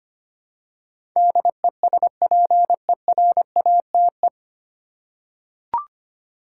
Morse code, heard as a keyed tone, about 25 wpm, 700 Hz. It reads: DESPERATE
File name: Nine Letter Words - 25wpm